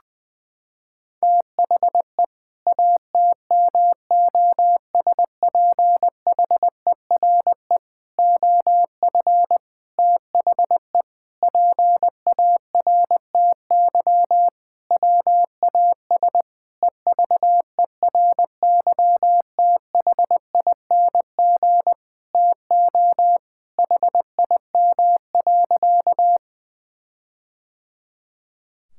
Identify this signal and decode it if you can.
Morse code, heard as a keyed tone, about 20 wpm, 700 Hz. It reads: THE ATMOSPHERE OF THE PARTY WAS EVERYTHING TO HIM.
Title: Sentences from Top 1600 Words 20wpm